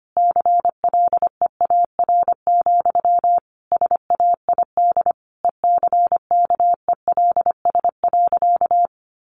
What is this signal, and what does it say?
Morse code, heard as a keyed tone, about 25 wpm, 700 Hz. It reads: CLEAR, SAID ECKELS.
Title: A Sound Of Thunder 25wpm